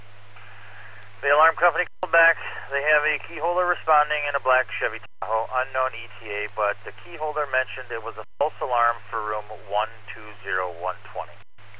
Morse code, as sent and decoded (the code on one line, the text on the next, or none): none